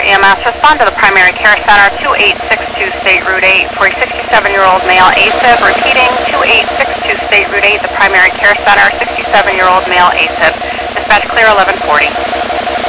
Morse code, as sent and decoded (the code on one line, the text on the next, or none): none